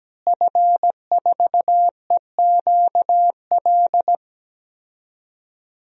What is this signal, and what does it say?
Morse code, heard as a keyed tone, about 17 wpm, 700 Hz. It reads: F4EQL
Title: DXCC-100 Call Signs - 5 Characters - Repeated 3x - 17wpm